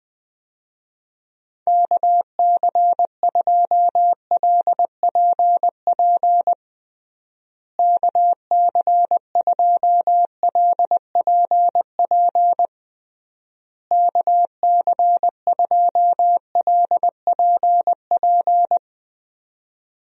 Morse code, heard as a keyed tone, about 20 wpm, 700 Hz. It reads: KC2LPP KC2LPP KC2LPP
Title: USA Call Signs - 6 Characters - Repeated 3x - 20wpm